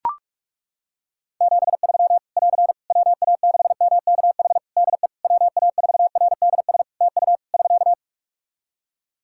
Morse code, heard as a keyed tone, about 41 wpm, 700 Hz. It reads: 73 <AR> WA6MKH DE WA4RDS TU <SK>
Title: Practice QSO - Exchange 4 - Ending - 45wpm